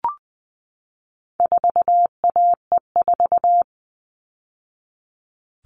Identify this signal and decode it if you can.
Morse code, heard as a keyed tone, about 20 wpm, 700 Hz. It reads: 4AE4
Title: All Cut Numbers - 4-Digit 20wpm